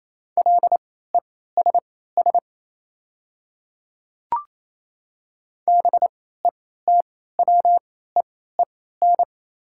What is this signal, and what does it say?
Morse code, heard as a keyed tone, about 28 wpm, spaced as for 16 wpm, 700 Hz. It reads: LESS BETWEEN
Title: 3x Character Spacing - Top 500 Words - 28wpm